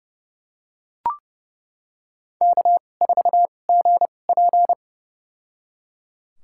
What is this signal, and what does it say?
Morse code, heard as a keyed tone, about 30 wpm, spaced as for 22 wpm, 700 Hz. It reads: K4GP
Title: USA Call Signs - 4 Characters - 2x Character Spacing - 30wpm